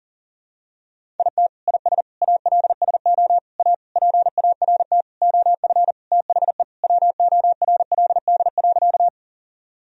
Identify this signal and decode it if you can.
Morse code, heard as a keyed tone, about 37 wpm, 700 Hz. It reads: IT IS ALSO A PART OF THE WORLD.
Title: Sentences from Top 1000 Words - Review Part 1 - 40wpm